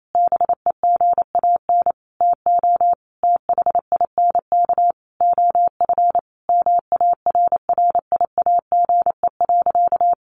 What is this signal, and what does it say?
Morse code, heard as a keyed tone, about 28 wpm, 700 Hz. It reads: BEGAN TO THINK OF MARRIAGE.
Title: Sentences from Top 800 Words 28wpm